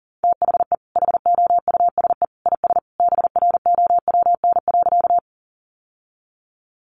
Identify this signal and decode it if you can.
Morse code, heard as a keyed tone, about 40 wpm, 700 Hz. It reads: THE HOUSE IS BROWN.
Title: Sentences from Top 1200 Words 40wpm